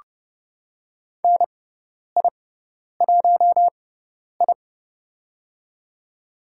Morse code, heard as a keyed tone, about 30 wpm, spaced as for 10 wpm, 700 Hz. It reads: NI1I
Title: USA Call Signs - 4 Characters - 6x Character Spacing - 30wpm